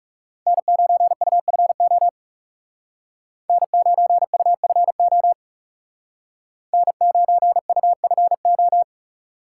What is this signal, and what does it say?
Morse code, heard as a keyed tone, about 39 wpm, 700 Hz. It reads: N9UFO N9UFO N9UFO